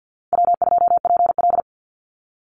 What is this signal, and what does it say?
Morse code, heard as a keyed tone, about 50 wpm, 700 Hz. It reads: W2PL
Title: Easy Call Signs Repeated 3x - 50wpm